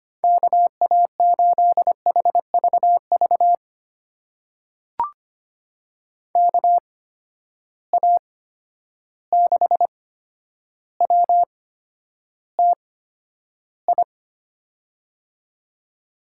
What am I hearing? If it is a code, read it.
Morse code, heard as a keyed tone, about 25 wpm, spaced as for 11 wpm, 700 Hz. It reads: KA8HVV KA6WTI
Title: USA Call Signs - 6 Characters - 8x Character Spacing - 25wpm